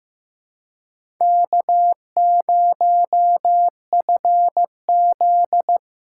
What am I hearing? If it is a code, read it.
Morse code, heard as a keyed tone, about 15 wpm, 700 Hz. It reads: K0FZ
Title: USA Call Signs - 4 Characters - 4x Character Spacing - 15wpm